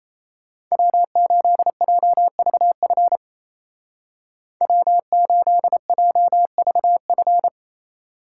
Morse code, heard as a keyed tone, about 35 wpm, 700 Hz. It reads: W8JVF W8JVF
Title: Speed Racing - USA 1x3 Call Signs 28wpm